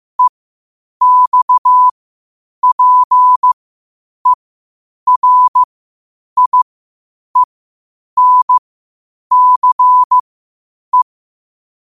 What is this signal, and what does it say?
Morse code, heard as a keyed tone, about 15 wpm, spaced as for 8 wpm, 1000 Hz. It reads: EXPERIENCE
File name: qotdfiles / 8wpm